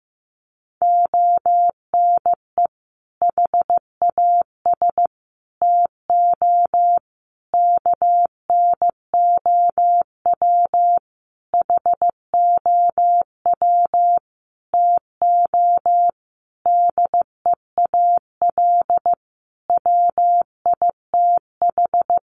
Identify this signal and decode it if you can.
Morse code, heard as a keyed tone, about 15 wpm, 700 Hz. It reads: ONE HAS TO KNOW HOW TO DEAL WITH
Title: Sentences from Top 800 Words 15wpm